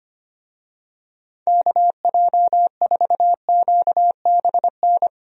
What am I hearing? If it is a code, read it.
Morse code, heard as a keyed tone, about 25 wpm, 700 Hz. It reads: KJ4QBN
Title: USA Call Signs - 6 Characters - Repeated 2x - 25wpm